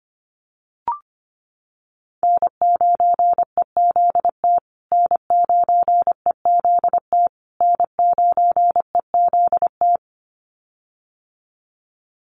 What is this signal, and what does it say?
Morse code, heard as a keyed tone, about 25 wpm, 700 Hz. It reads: N9EZT N9EZT N9EZT
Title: USA Call Signs Repeated 3x 25wpm